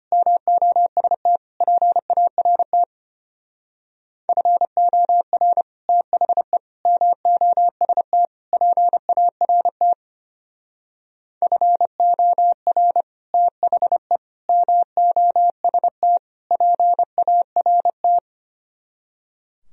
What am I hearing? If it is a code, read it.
Morse code, heard as a keyed tone, about 31 wpm, 700 Hz. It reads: MOST PART FOR THE MOST PART FOR THE MOST PART
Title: Speed-Racing - Sets of 4 Words 25wpm